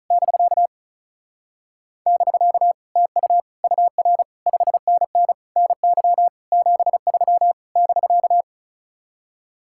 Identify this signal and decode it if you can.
Morse code, heard as a keyed tone, about 36 wpm, 700 Hz. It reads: <BK> <BK> TU UR 5NN NY 73 <BK>
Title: Speed-Racing Common QSO Phrases - 35wpm